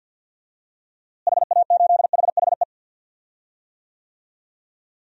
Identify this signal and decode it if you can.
Morse code, heard as a keyed tone, about 48 wpm, 700 Hz. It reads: HA8HHE
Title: DXCC-100 Call Signs - 6 Characters - Repeated 2x - 50wpm